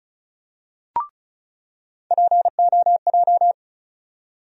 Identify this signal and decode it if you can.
Morse code, heard as a keyed tone, about 36 wpm, 700 Hz. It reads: POJ
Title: USA Call Sign Postfixes - 3 Letters 35wpm